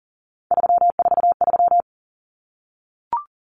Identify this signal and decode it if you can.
Morse code, heard as a keyed tone, about 40 wpm, 700 Hz. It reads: 343
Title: Common Cut Numbers - 3-Digit 40wpm